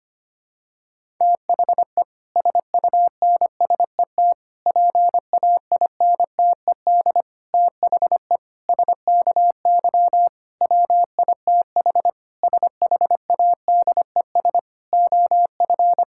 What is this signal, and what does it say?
Morse code, heard as a keyed tone, about 25 wpm, 700 Hz. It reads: THE SUNSET PAINTED THE SKY WITH SHADES OF
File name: practice / 25wpm / Difficult Sentences with Hints - 25wpm